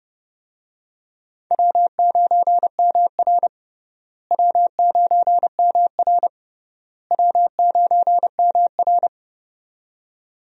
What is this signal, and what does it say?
Morse code, heard as a keyed tone, about 30 wpm, 700 Hz. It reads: W9MR W9MR W9MR
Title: USA Call Signs - 4 Characters - Repeated 3x - 30wpm